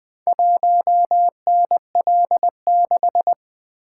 Morse code, heard as a keyed tone, about 20 wpm, 700 Hz. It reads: 1NL6
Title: Four Letter-Number Combinations - 20wpm